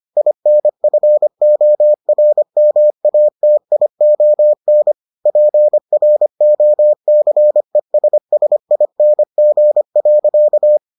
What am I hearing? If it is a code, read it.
Morse code, heard as a keyed tone, about 25 wpm, 600 Hz. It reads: INFORMATION PROCESSING.